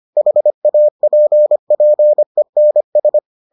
Morse code, heard as a keyed tone, about 25 wpm, 600 Hz. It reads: HAPPENS